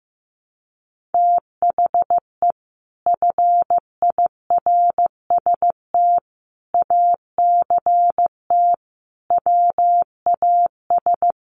Morse code, heard as a keyed tone, about 15 wpm, 700 Hz. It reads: THE FIRST ACT WAS
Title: Sentences from Top 200 Words 15wpm